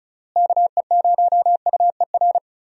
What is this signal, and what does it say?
Morse code, heard as a keyed tone, about 34 wpm, 700 Hz. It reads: KE0UER
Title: USA Call Signs - 6 Characters - Repeated 3x - 35wpm